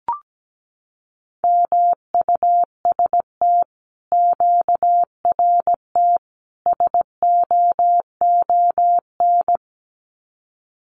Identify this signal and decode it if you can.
Morse code, heard as a keyed tone, about 17 wpm, 700 Hz. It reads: MUST QRT SOON
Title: Common QSO Phrases Encore 17wpm